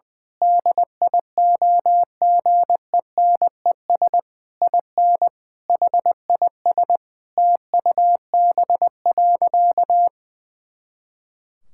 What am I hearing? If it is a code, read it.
Morse code, heard as a keyed tone, about 20 wpm, 700 Hz. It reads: DIOGENES IN HIS TUB.
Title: Pangram Challenge v2 - 20wpm